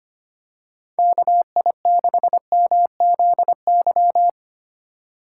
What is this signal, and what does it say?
Morse code, heard as a keyed tone, about 25 wpm, 700 Hz. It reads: KI6MZY
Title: USA Call Signs - 6 Characters - 4x Character Spacing - 25wpm